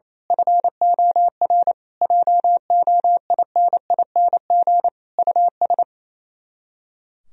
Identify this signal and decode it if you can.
Morse code, heard as a keyed tone, about 26 wpm, 700 Hz. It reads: FOR JOINING US
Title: Speed-Racing - Sets of 5 Words 28wpm